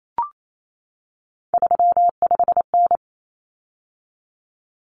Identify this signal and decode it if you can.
Morse code, heard as a keyed tone, about 28 wpm, 700 Hz. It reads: 35N
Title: Common Cut Numbers - 3-Digit 28wpm